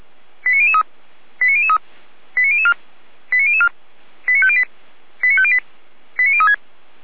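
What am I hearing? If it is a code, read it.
Five-tone selective call (ZVEI), heard as quick runs of five tones, tones of about 70 ms. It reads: 89003 89003 89004 89004 89508 89508 89036
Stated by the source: Five-tone ZVEI selective call received on an SDR, from audio.f85360.feuerwehr-orlamuende.org, a fire brigade (Feuerwehr Orlamünde) server.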